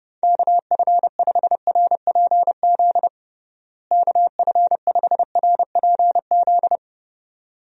Morse code, heard as a keyed tone, about 30 wpm, 700 Hz. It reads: KF5RPZ KF5RPZ
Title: USA Call Signs - 6 Characters - Repeated 2x - 30wpm